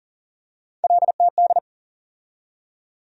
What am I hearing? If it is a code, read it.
Morse code, heard as a keyed tone, about 40 wpm, 700 Hz. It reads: LTD
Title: USA Call Sign Postfixes - 3 Letters 40wpm